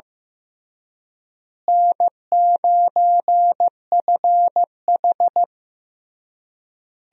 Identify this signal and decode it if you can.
Morse code, heard as a keyed tone, about 16 wpm, 700 Hz. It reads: N9FH
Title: USA Call Signs - 4 Characters - Repeated 3x - 15wpm